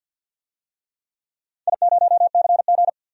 Morse code, heard as a keyed tone, about 50 wpm, 700 Hz. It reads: I0CG
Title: DXCC-100 Call Signs - 4 Characters - Repeated 2x - 50wpm